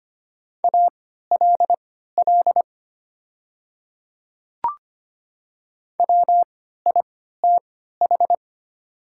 Morse code, heard as a keyed tone, about 25 wpm, spaced as for 14 wpm, 700 Hz. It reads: ALL WITH
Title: 3x Character Spacing - Top 100 Words - 25wpm